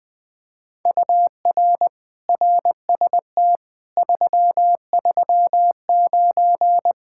Morse code, heard as a keyed tone, about 21 wpm, 700 Hz. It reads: UR RST 339
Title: Speed-Racing Common QSO Phrases - 20wpm